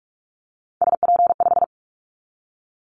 Morse code, heard as a keyed tone, about 46 wpm, 700 Hz. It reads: SP5
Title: Callsign Prefixes for Top 25 DXCC - 45wpm